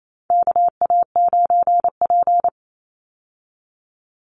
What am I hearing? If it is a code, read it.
Morse code, heard as a keyed tone, about 28 wpm, 700 Hz. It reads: KA9P
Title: Call Signs Repeated 2x - 28wpm